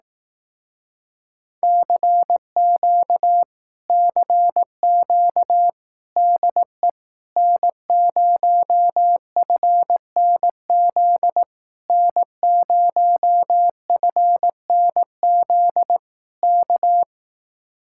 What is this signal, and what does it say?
Morse code, heard as a keyed tone, about 18 wpm, 700 Hz. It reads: CQ CQ DE N0FNZ N0FNZ K
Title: Speed-Racing Common QSO Phrases Encore - 15wpm